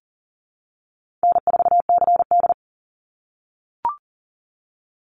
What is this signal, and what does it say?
Morse code, heard as a keyed tone, about 37 wpm, 700 Hz. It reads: N4CD
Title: Easy Call Signs - 40wpm